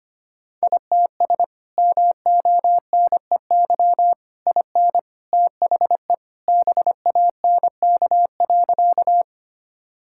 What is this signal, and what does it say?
Morse code, heard as a keyed tone, about 25 wpm, 700 Hz. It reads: ITS MONEY IN THE BANK.
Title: Sentences from Top 1000 Words - Review Part 6 - 25wpm